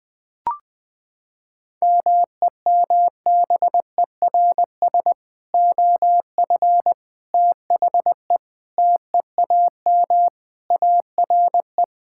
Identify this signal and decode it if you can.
Morse code, heard as a keyed tone, about 20 wpm, 700 Hz. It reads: MEMBERS OF THE TEAM ARE
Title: Sentences with 3-Word Phrases - 20wpm